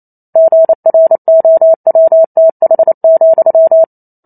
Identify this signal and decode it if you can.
Morse code, heard as a keyed tone, about 28 wpm, 650 Hz. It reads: GROWTH,